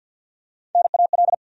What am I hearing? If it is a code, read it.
Morse code, heard as a keyed tone, about 49 wpm, 700 Hz. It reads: NAL
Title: USA Call Sign Postfixes - 3 Letters 50wpm